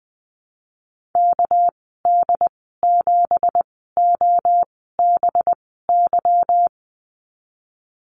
Morse code, heard as a keyed tone, about 19 wpm, spaced as for 14 wpm, 700 Hz. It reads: KD7OBY
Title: 238 Lesson 38g - Letter q - Call Signs 20wpm